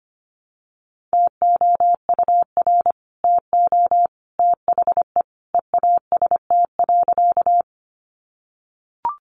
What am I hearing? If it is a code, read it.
Morse code, heard as a keyed tone, about 25 wpm, 700 Hz. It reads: TOUR TO THE EAST.
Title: Sentences from Top 1400 Words 25wpm